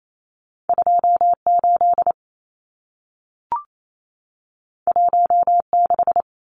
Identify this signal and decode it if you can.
Morse code, heard as a keyed tone, about 28 wpm, 700 Hz. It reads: 28 16
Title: Numbers - 2-Digit 28wpm